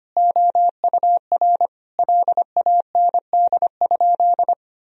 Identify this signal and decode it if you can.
Morse code, heard as a keyed tone, about 25 wpm, 700 Hz. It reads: OUR LAND?
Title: Sentences from Top 1000 Words - Review Part 4 - 25wpm